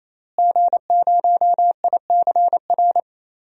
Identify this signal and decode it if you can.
Morse code, heard as a keyed tone, about 28 wpm, 700 Hz. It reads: G0ICR
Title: DXCC-100 Call Signs - 5 Characters - Repeated 3x - 28wpm